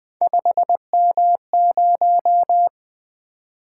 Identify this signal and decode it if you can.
Morse code, heard as a keyed tone, about 20 wpm, 700 Hz. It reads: 5M0
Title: Three Letter-Number Combinations - 20wpm